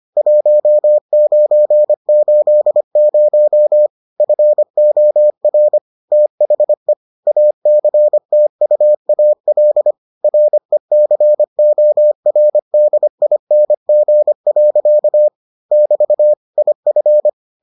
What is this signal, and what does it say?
Morse code, heard as a keyed tone, about 25 wpm, 600 Hz. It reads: 1980 FOR THE ACTUAL RECORDING. <BT> IF